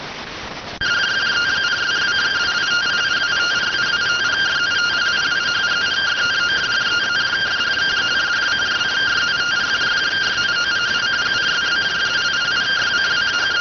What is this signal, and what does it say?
Two-tone teleprinter signal, about 170 Hz shift.